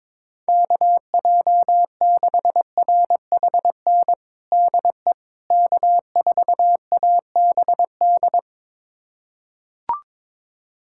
Morse code, heard as a keyed tone, about 22 wpm, 700 Hz. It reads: KJ6RHN DE K4ABD
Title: USA Callsign DE Callsign 22wpm